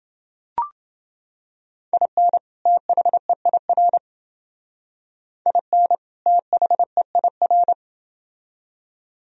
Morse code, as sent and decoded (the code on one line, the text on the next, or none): .. -. / - .... .. .-. / .. -. / - .... .. .-.
IN THEIR IN THEIR